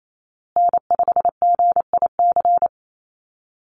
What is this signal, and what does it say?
Morse code, heard as a keyed tone, about 28 wpm, 700 Hz. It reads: N5GIC
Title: USA Call Signs Repeated 2x 28wpm